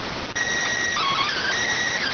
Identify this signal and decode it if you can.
SSTV (slow-scan television), PD 290.